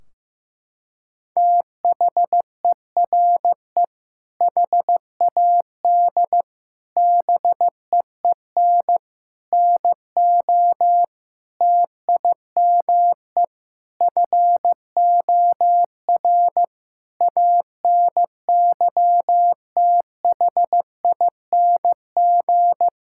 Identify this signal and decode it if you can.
Morse code, heard as a keyed tone, about 15 wpm, 700 Hz. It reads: THERE HAD BEEN NO TIME FOR ANYTHING
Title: Sentences from Top 1600 Words 15wpm